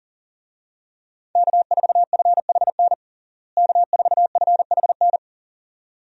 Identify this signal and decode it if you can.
Morse code, heard as a keyed tone, about 40 wpm, 700 Hz. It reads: K4FHN K4FHN
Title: USA Call Signs - 5 Characters - Repeated 2x - 40wpm